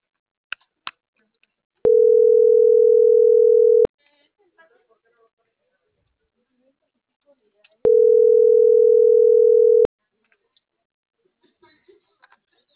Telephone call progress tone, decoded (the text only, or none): ringback tone